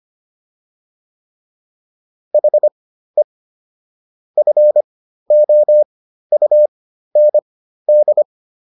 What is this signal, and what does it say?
Morse code, heard as a keyed tone, about 25 wpm, spaced as for 13 wpm, 600 Hz. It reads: HE FOUND